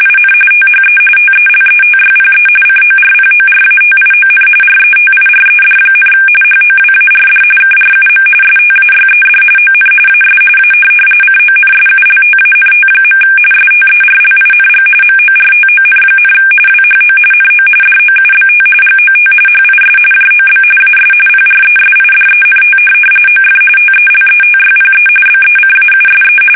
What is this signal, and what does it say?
RTTY, about 45.45 baud. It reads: IN THE TELETYPE CORPORATION RETIRED, HOWARD KRUM STAYED ON AS A CONSULTANT. STERLING MORTON, WHO NO DOUBT GOT HIS JOB AS PRESIDENT OF TELETYPE BECAUSE
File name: fsk850 / teletype-corp